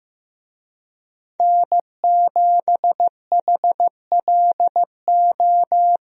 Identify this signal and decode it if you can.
Morse code, heard as a keyed tone, about 15 wpm, 700 Hz. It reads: N7HLO
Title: USA Call Signs - 5 Characters - Repeated 2x - 15wpm